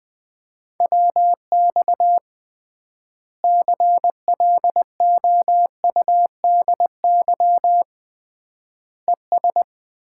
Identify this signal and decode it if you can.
Morse code, heard as a keyed tone, about 20 wpm, 700 Hz. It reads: WX CLOUDY ES
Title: Common QSO Phrases - 3x Word-Spacing - 20wpm